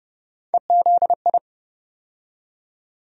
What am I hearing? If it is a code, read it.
Morse code, heard as a keyed tone, about 30 wpm, 700 Hz. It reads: EZI
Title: USA Call Sign Postfixes - 3 Letters 30wpm